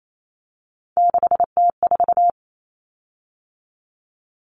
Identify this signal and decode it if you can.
Morse code, heard as a keyed tone, about 28 wpm, 700 Hz. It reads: 6T4
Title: Common Cut Numbers - 3-Digit 28wpm